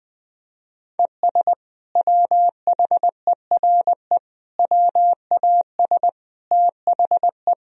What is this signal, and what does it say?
Morse code, heard as a keyed tone, about 21 wpm, 700 Hz. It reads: ES WHERE WAS THE
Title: Sentences from Top 200 Words and Abbreviations 20wpm